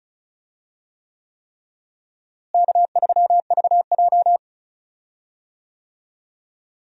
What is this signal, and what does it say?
Morse code, heard as a keyed tone, about 36 wpm, 700 Hz. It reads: K3VJ
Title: USA Call Signs - 4 Characters - Repeated 3x - 35wpm